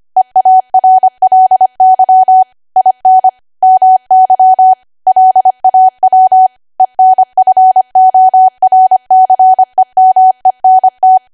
Morse code, heard as a keyed tone, about 25 wpm, 750 Hz. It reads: EARLY IN MY LAW ENFORCEMENT
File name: Archive / 25 WPM / 25WPM